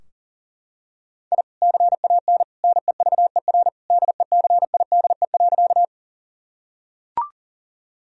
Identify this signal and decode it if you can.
Morse code, heard as a keyed tone, about 37 wpm, 700 Hz. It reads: I CAN NEVER DECIDE.
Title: Sentences from Top 1000 Words - Review Part 3 - 40wpm